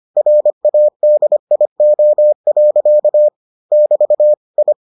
Morse code, heard as a keyed tone, about 25 wpm, 600 Hz. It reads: RADIO. <BT> I